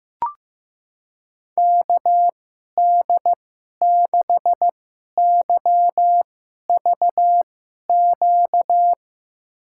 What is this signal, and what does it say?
Morse code, heard as a keyed tone, about 15 wpm, spaced as for 11 wpm, 700 Hz. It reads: KD6YVQ
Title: USA Call Signs - 6 Characters - 2x Character Spacing - 15wpm